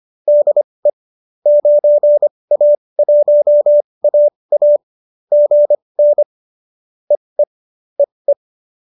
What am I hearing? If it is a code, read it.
Morse code, heard as a keyed tone, about 25 wpm, spaced as for 20 wpm, 590 Hz. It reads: DE 9A1AA GN EE EE